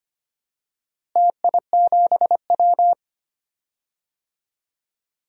Morse code, heard as a keyed tone, about 25 wpm, 700 Hz. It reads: TI7W
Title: DXCC-100 Call Signs - 4 Characters - Repeated 3x - 25wpm